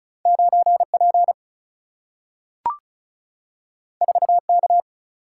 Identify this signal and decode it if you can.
Morse code, heard as a keyed tone, about 36 wpm, 700 Hz. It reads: 9P 4K
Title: Single-Letter Single-Number 35wpm